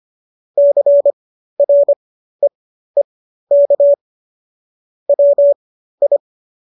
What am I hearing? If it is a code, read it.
Morse code, heard as a keyed tone, about 25 wpm, spaced as for 13 wpm, 570 Hz. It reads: CREEK WI